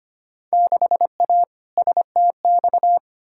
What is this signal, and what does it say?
Morse code, heard as a keyed tone, about 25 wpm, 700 Hz. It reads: 6A STX
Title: Field Day Exchanges - 25wpm